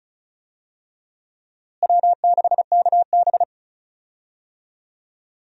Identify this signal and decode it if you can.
Morse code, heard as a keyed tone, about 36 wpm, 700 Hz. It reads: W6KB
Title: USA Call Signs - 4 Characters - Repeated 3x - 35wpm